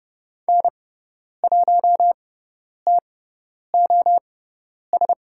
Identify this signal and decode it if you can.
Morse code, heard as a keyed tone, about 30 wpm, spaced as for 10 wpm, 700 Hz. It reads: N1TOS
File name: Lesson 16g - Letter c - Call Signs 30wpm